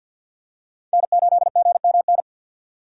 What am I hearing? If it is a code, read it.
Morse code, heard as a keyed tone, about 51 wpm, 700 Hz. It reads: N8GMN
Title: USA Call Signs - 5 Characters - 2x Character Spacing - 50wpm